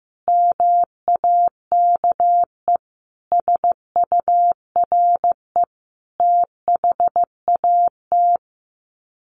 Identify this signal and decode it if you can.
Morse code, heard as a keyed tone, about 15 wpm, 700 Hz. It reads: MAKE SURE THAT
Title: Sets of 5 Words 15wpm